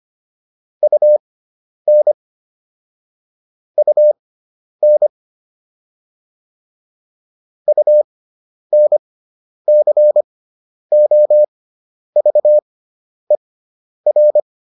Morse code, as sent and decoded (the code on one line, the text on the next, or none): ..- -. / ..- -. / ..- -. -.-. --- ...- . .-.
UN UN UNCOVER